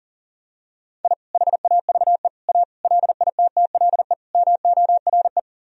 Morse code, heard as a keyed tone, about 40 wpm, 700 Hz. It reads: I HAVE A LITTLE MORE
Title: Sentences from Top 1000 Words - Review Part 1 - 40wpm